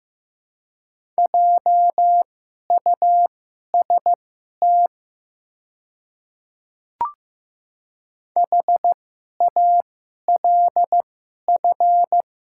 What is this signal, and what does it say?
Morse code, heard as a keyed tone, about 15 wpm, spaced as for 11 wpm, 700 Hz. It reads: JUST HALF
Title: 2x Character Spacing - Top 500 Words - 15wpm